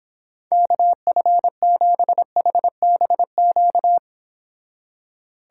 Morse code, heard as a keyed tone, about 26 wpm, 700 Hz. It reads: KF7HBQ